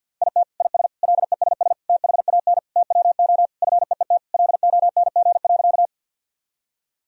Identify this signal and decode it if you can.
Morse code, heard as a keyed tone, about 50 wpm, 700 Hz. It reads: IT IS LESS THAN TWO FEET LONG.
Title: Sentences from Top 1000 Words - Review Part 2 - 50wpm